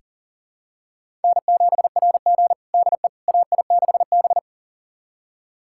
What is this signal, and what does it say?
Morse code, heard as a keyed tone, about 40 wpm, 700 Hz. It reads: N7RG DE AI6B
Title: USA Callsign DE Callsign - 4 Characters - 40wpm